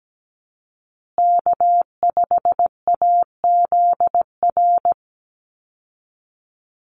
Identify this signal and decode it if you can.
Morse code, heard as a keyed tone, about 17 wpm, 700 Hz. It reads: K5AZR
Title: Call Signs Repeated 2x - 17wpm